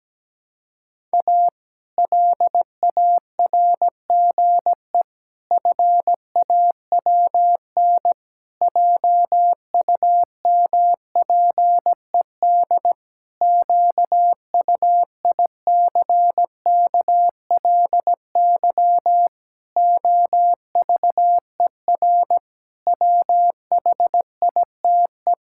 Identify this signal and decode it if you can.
Morse code, heard as a keyed tone, about 17 wpm, 700 Hz. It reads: A LARGE FAWN JUMPED QUICKLY OVER WHITE
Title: Pangram Challenge v2 - 17wpm